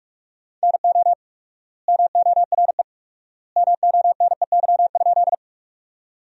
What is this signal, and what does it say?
Morse code, heard as a keyed tone, about 45 wpm, 700 Hz. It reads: NO MORE MONEY?
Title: 4x Word-Spacing - Sentences from Top 400 Words - 45wpm